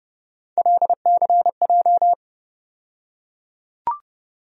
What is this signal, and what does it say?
Morse code, heard as a keyed tone, about 30 wpm, 700 Hz. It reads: LCJ
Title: USA Call Sign Postfixes - 3 Letters 30wpm